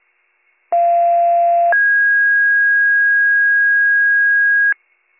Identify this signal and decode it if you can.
Two-tone page, heard as a one-second tone, then a three-second tone, about 690 then 1650 Hz. It is heard from 0.7 to 4.7 s.